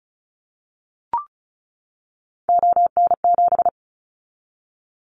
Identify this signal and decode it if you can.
Morse code, heard as a keyed tone, about 34 wpm, 700 Hz. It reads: ON7
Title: Callsign Prefixes for Top 50 DXCC - 35wpm